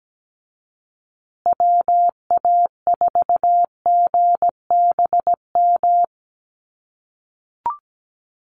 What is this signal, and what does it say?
Morse code, heard as a keyed tone, about 17 wpm, 700 Hz. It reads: WA4GBM
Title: USA Call Signs Repeated 3x 17wpm